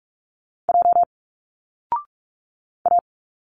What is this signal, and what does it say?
Morse code, heard as a keyed tone, about 46 wpm, 700 Hz. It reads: J A
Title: Single Letters 45wpm